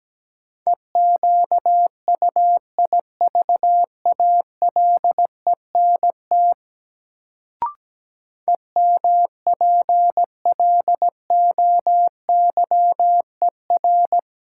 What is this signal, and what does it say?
Morse code, heard as a keyed tone, about 17 wpm, 700 Hz. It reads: EQUIVALENT EMPLOYER
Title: Sentences from Top 1600 Words - New Word Drill - 17wpm